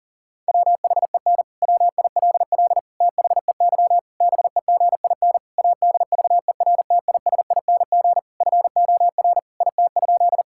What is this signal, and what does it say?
Morse code, heard as a keyed tone, about 40 wpm, 700 Hz. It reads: WHEN WILL THEY BEGIN ADVERTISING FOR IT?
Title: Sentences from Top 1000 Words - Review Part 6 - 40wpm